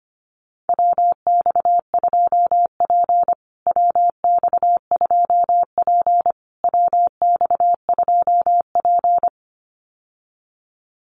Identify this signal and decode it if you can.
Morse code, heard as a keyed tone, about 25 wpm, 700 Hz. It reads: WX2P WX2P WX2P
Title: USA Call Signs Repeated 3x 25wpm